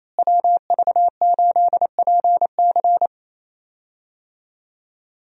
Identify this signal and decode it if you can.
Morse code, heard as a keyed tone, about 28 wpm, 700 Hz. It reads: WV8PC
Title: USA Call Signs - 5 Characters - Repeated 3x - 28wpm